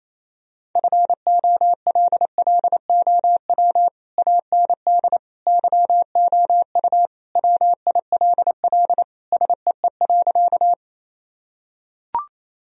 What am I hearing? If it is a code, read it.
Morse code, heard as a keyed tone, about 28 wpm, 700 Hz. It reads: FOLLOW AND YOU WILL SEE.
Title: Sentences from Top 1000 Words - Review Part 1 - 28wpm